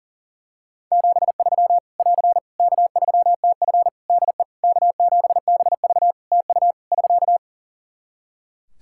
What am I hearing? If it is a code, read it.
Morse code, heard as a keyed tone, about 37 wpm, 700 Hz. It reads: 73 <AR> K3TF DE K7BV TU <SK>
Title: Practice QSO - Exchange 4 - Ending - 40wpm